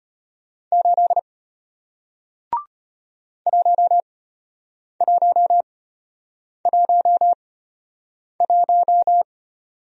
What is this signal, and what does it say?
Morse code, heard as a keyed tone, about 33 wpm, spaced as for 8 wpm, 700 Hz. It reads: 8 1111